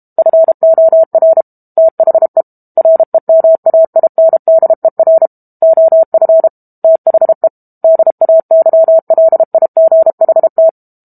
Morse code, heard as a keyed tone, about 32 wpm, 650 Hz. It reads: FOR THE REMAINDER OF THE DAYLIGHT
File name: Book Two / 06-Chapter 6 - 32 WPM